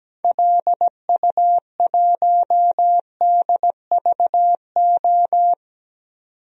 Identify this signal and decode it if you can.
Morse code, heard as a keyed tone, about 17 wpm, 700 Hz. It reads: LU1DVO
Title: DXCC-100 Call Signs - 6 Characters - 17wpm